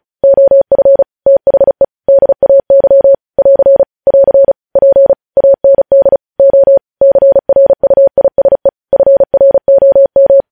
Morse code, heard as a keyed tone, about 36 wpm, 550 Hz. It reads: OF THE DAY <AR> <AR> P AND O CRUISE FROM
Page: news headlines in morse code at 35 wpm